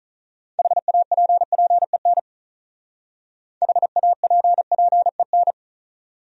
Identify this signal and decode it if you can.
Morse code, heard as a keyed tone, about 39 wpm, 700 Hz. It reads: HAPPEN HAPPEN
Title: Speed-Racing Top 500 Words 35wpm